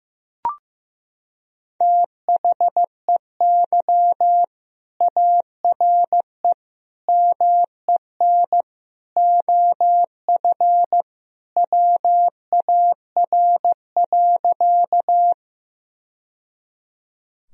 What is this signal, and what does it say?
Morse code, heard as a keyed tone, about 15 wpm, 700 Hz. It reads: THEY ARE MEN OF WAR.
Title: Sentences from Top 1000 Words - Review Part 2 - 15wpm